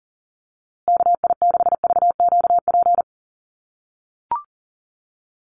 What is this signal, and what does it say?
Morse code, heard as a keyed tone, about 40 wpm, 700 Hz. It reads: KI6VQP
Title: USA Call Signs Repeated 3x 40wpm